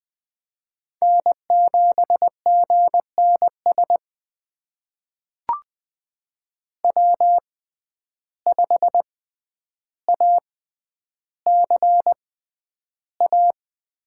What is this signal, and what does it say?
Morse code, heard as a keyed tone, about 20 wpm, spaced as for 10 wpm, 700 Hz. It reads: N7GNS W5ACA
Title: USA Call Signs - 6x Character Spacing - 20wpm